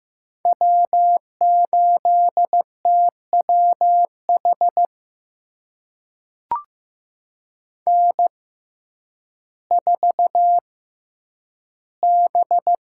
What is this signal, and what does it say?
Morse code, heard as a keyed tone, about 15 wpm, spaced as for 9 wpm, 700 Hz. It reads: W8TWH N4B
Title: USA Call Signs - 6x Character Spacing - 15wpm